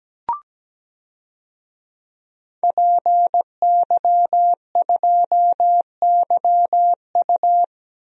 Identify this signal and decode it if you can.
Morse code, heard as a keyed tone, about 17 wpm, 700 Hz. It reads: PY2YU
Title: DXCC-100 Call Signs - 5 Characters - Repeated 3x - 17wpm